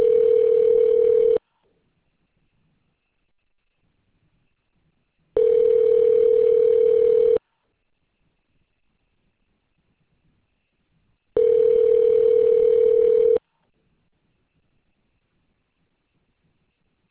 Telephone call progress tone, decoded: ringback tone

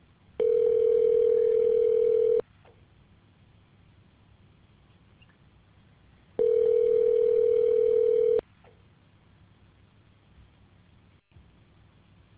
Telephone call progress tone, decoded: ringback tone

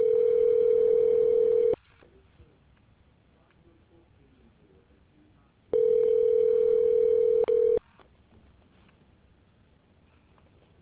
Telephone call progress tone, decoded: ringback tone

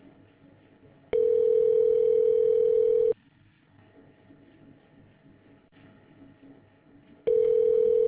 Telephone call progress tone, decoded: ringback tone

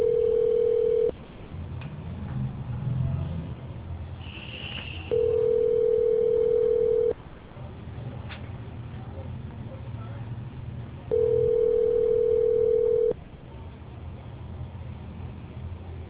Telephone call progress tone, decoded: ringback tone

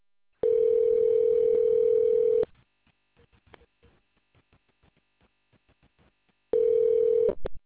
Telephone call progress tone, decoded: ringback tone

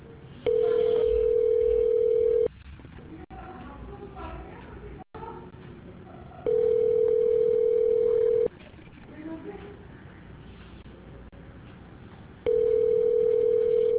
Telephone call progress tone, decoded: ringback tone

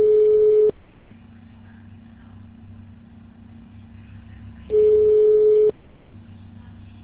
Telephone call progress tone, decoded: ringback tone

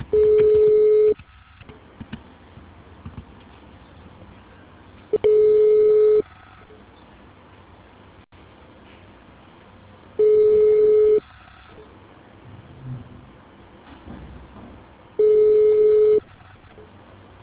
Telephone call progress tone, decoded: ringback tone

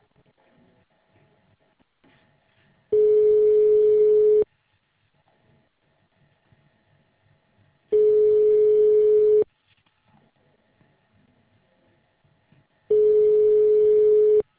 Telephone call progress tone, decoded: ringback tone

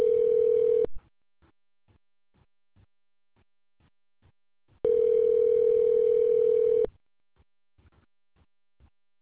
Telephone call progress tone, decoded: ringback tone

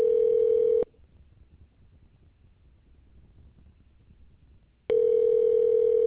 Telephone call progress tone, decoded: ringback tone

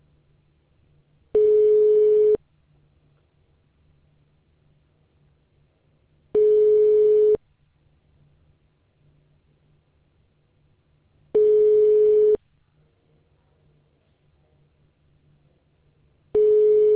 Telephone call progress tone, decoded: ringback tone

